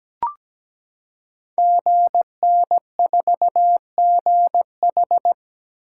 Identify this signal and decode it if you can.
Morse code, heard as a keyed tone, about 17 wpm, 700 Hz. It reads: GN4GH
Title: Code Groups - 5 Letter:Numbers - 17wpm